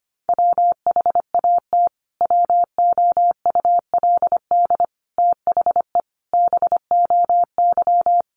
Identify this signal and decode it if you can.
Morse code, heard as a keyed tone, about 25 wpm, 700 Hz. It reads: WHAT WOULD THE BOY
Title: Sentences from Top 1300 Words 25wpm